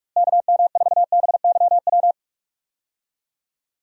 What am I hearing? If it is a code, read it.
Morse code, heard as a keyed tone, about 41 wpm, 700 Hz. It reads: KM4BYW
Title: USA Call Signs - 6 Characters - Repeated 3x - 45wpm